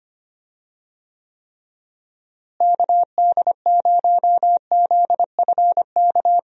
Morse code, heard as a keyed tone, about 25 wpm, 700 Hz. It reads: KD0ZFK
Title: USA Call Signs - 6 Characters - Repeated 3x - 25wpm